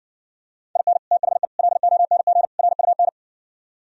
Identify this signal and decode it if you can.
Morse code, heard as a keyed tone, about 59 wpm, 700 Hz. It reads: IN THE LONG RUN